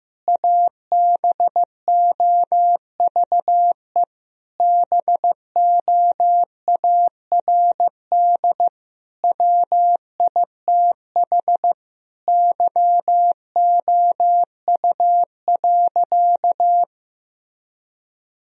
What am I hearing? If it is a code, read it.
Morse code, heard as a keyed tone, about 16 wpm, 700 Hz. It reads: ABOVE BOARD WITH YOU.
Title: Sentences from Top 1000 Words - Review Part 4 - 15wpm